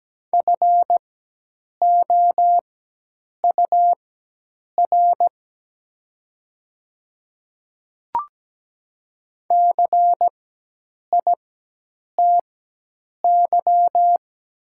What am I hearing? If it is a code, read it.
Morse code, heard as a keyed tone, about 17 wpm, spaced as for 8 wpm, 700 Hz. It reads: FOUR CITY
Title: 4x Character Spacing - Top 300 Words - 17wpm